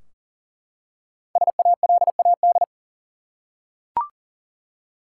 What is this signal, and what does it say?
Morse code, heard as a keyed tone, about 40 wpm, 700 Hz. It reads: SALAD
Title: Five Letter Words - 40wpm